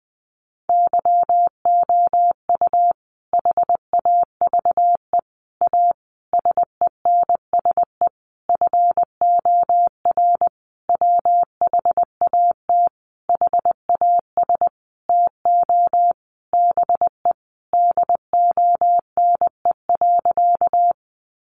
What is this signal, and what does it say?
Morse code, heard as a keyed tone, about 19 wpm, 700 Hz. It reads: YOU HAVE A SENSE FOR WHAT HAS TO BE DONE.
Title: Sentences from Top 700 Words 20wpm